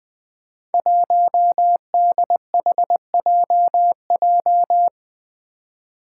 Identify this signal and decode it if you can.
Morse code, heard as a keyed tone, about 20 wpm, 700 Hz. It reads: 1DHJJ
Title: Five Challenging Letter-Number Combinations 20wpm